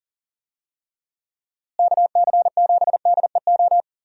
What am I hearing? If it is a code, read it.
Morse code, heard as a keyed tone, about 40 wpm, 700 Hz. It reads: KC7DEO
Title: USA Call Signs - 6 Characters - Repeated 2x - 40wpm